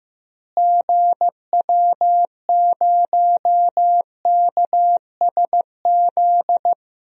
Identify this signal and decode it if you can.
Morse code, heard as a keyed tone, about 15 wpm, 700 Hz. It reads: GW0KSZ
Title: DXCC-100 Call Signs - 6 Characters - Repeated 3x - 15wpm